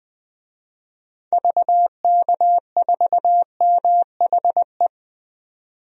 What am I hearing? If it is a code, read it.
Morse code, heard as a keyed tone, about 20 wpm, 700 Hz. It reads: VK4MHE
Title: DXCC-100 Call Signs - 6 Characters - 20wpm